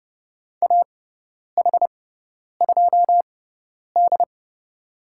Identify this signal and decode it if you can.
Morse code, heard as a keyed tone, about 30 wpm, spaced as for 10 wpm, 700 Hz. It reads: AH2D
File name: Lesson 16g - Letter c - Call Signs 30wpm